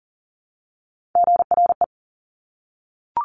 Morse code, heard as a keyed tone, about 38 wpm, 700 Hz. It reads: GRE
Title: Top Three Letter Combinations in Words 40wpm